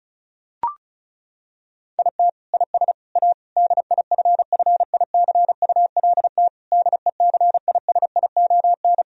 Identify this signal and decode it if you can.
Morse code, heard as a keyed tone, about 34 wpm, 700 Hz. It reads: IT IS A DIFFICULT DECISION
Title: Sentences from Top 1000 Words - Review Part 4 - 35wpm